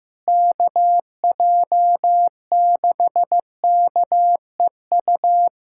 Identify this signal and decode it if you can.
Morse code, heard as a keyed tone, about 15 wpm, 700 Hz. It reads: KJ6KEU
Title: USA Callsign DE Callsign - 6 Characters - 15wpm